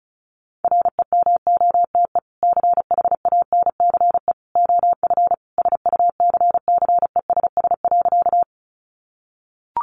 Morse code, heard as a keyed tone, about 34 wpm, 700 Hz. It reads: REMOTE CHANCE OF SUCCESS.
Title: Sentences from Top 1300 Words 35wpm